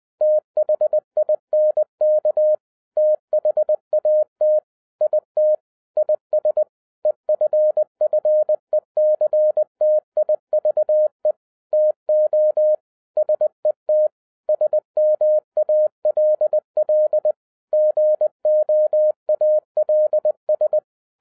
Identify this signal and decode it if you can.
Morse code, heard as a keyed tone, about 20 wpm, 610 Hz. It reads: THINK THAT IT IS EFFECTIVE TO SET SMALL GOALS
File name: 2011-10-mp3 / n13-TIPS FROM MY 34 YEARS OF EXPERIENCE 20wpm